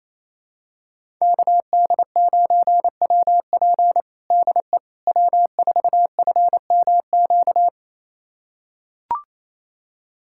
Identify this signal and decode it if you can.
Morse code, heard as a keyed tone, about 28 wpm, 700 Hz. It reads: KD9WP DE W4FMQ
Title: USA Callsign DE Callsign - 5 Characters - 28wpm